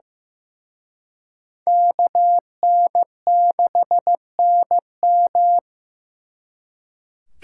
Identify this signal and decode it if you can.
Morse code, heard as a keyed tone, about 18 wpm, 700 Hz. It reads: KN6NM
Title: Speed Racing - USA 2x2 Call Signs 15wpm